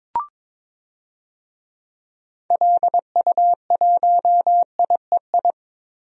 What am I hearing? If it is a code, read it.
Morse code, heard as a keyed tone, about 21 wpm, 700 Hz. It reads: LU1IEI
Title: DXCC-100 Call Signs - 6 Characters - Repeated 2x - 22wpm